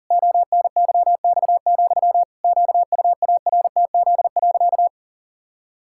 Morse code, heard as a keyed tone, about 37 wpm, 700 Hz. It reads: ONYX, QUARTZ.
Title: Pangram Challenge v2 - 40wpm